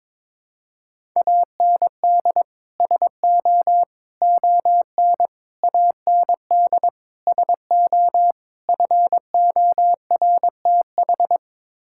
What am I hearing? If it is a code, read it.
Morse code, heard as a keyed tone, about 22 wpm, 700 Hz. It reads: AND SO ON AND SO FORTH
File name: practice / 22wpm / Sets of 6 Words - 22wpm